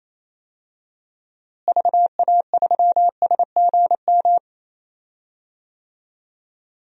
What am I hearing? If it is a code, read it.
Morse code, heard as a keyed tone, about 28 wpm, 700 Hz. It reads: VA3SGM
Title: DXCC-100 Call Signs - 6 Characters - Repeated 2x - 28wpm